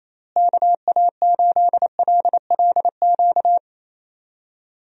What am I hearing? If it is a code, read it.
Morse code, heard as a keyed tone, about 28 wpm, 700 Hz. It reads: KA8LLQ
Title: USA Call Signs - 6 Characters - Repeated 3x - 28wpm